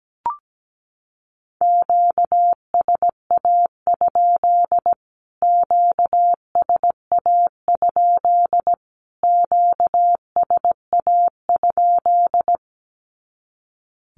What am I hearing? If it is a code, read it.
Morse code, heard as a keyed tone, about 17 wpm, 700 Hz. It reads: QSA? QSA? QSA?
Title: Q-Codes Repeated 3x 17wpm